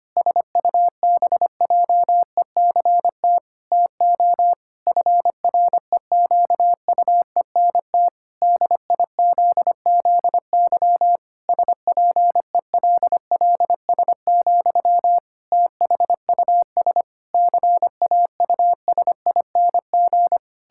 Morse code, heard as a keyed tone, about 25 wpm, 700 Hz. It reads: SUBJECT TO FREQUENT DIZZY SPELLS, THUS CAUSING
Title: Pangram Challenge v2 - 25wpm